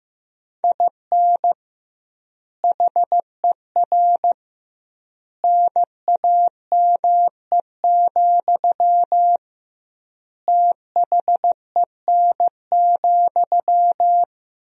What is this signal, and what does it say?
Morse code, heard as a keyed tone, about 15 wpm, 700 Hz. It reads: IN HER NAME, THEN,
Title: 2x Word-Spacing - Sentences from Top 500 Words - 15wpm